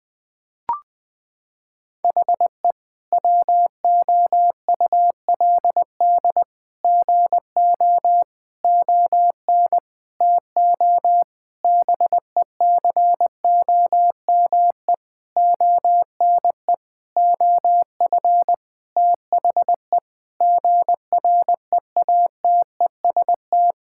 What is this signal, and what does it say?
Morse code, heard as a keyed tone, about 21 wpm, 700 Hz. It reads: HE WOULD GO ON TO BECOME ONE OF THE GREATEST
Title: Sentences with 8-Word Phrases - 20wpm